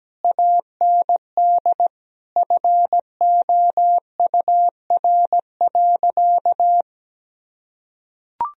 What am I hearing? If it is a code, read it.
Morse code, heard as a keyed tone, about 17 wpm, 700 Hz. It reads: AND FOUR.
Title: Sentences from Top 1000 Words - Review Part 2 - 17wpm